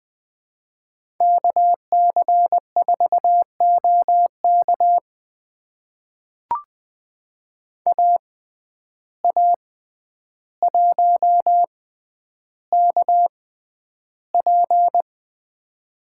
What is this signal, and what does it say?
Morse code, heard as a keyed tone, about 20 wpm, spaced as for 10 wpm, 700 Hz. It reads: KC4OK AA1KP
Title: USA Call Signs - 5 Characters - 6x Character Spacing - 20wpm